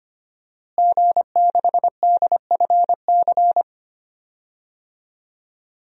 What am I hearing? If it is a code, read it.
Morse code, heard as a keyed tone, about 25 wpm, 700 Hz. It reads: G6DFC
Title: DXCC-100 Call Signs - 5 Characters - Repeated 3x - 25wpm